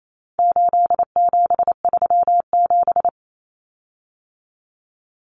Morse code, heard as a keyed tone, about 28 wpm, 700 Hz. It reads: 8737
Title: Common Cut Numbers - 4-Digit 28wpm